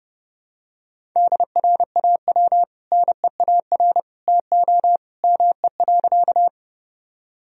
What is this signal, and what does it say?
Morse code, heard as a keyed tone, about 30 wpm, 700 Hz. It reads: DRAW NEAR TO ME.
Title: Sentences from Top 1000 Words - Review Part 1 - 30wpm